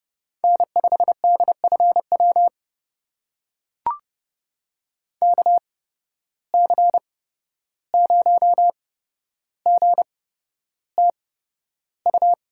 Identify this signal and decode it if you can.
Morse code, heard as a keyed tone, about 31 wpm, spaced as for 12 wpm, 700 Hz. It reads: N5DFW KC0GTU
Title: USA Call Signs - 8x Character Spacing - 30wpm